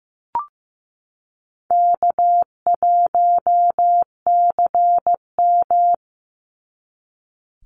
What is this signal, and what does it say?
Morse code, heard as a keyed tone, about 15 wpm, 700 Hz. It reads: K1CM
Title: NA SOTA Chaser Callsigns 15wpm